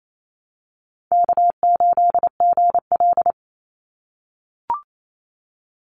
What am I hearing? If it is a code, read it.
Morse code, heard as a keyed tone, about 28 wpm, 700 Hz. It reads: K8GL
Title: Easy Call Signs - 28wpm